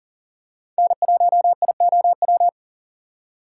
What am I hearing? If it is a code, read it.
Morse code, heard as a keyed tone, about 40 wpm, 700 Hz. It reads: N1IOW
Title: USA Call Signs - 6x Character Spacing - 40wpm